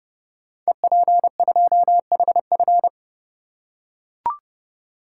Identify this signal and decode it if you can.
Morse code, heard as a keyed tone, about 30 wpm, 700 Hz. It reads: EP2HF
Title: Code Groups - 5 Letter:Numbers - 30wpm